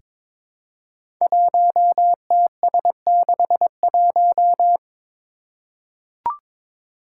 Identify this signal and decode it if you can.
Morse code, heard as a keyed tone, about 22 wpm, 700 Hz. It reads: 1TS61
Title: Five Challenging Letter-Number Combinations 22wpm